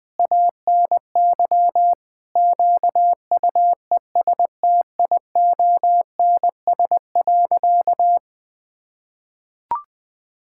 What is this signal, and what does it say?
Morse code, heard as a keyed tone, about 20 wpm, 700 Hz. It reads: ANY QUESTIONS.
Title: Sentences with 7-Word Phrases - 20wpm